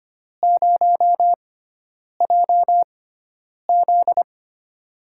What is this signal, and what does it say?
Morse code, heard as a keyed tone, about 25 wpm, spaced as for 9 wpm, 700 Hz. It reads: 0JZ
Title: USA Call Signs - 4 Characters - 6x Character Spacing - 25wpm